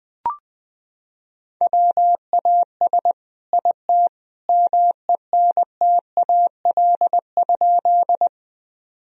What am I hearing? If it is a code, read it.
Morse code, heard as a keyed tone, about 20 wpm, 700 Hz. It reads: WAS IT MENTAL?
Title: Sentences from Top 1000 Words - Review Part 4 - 20wpm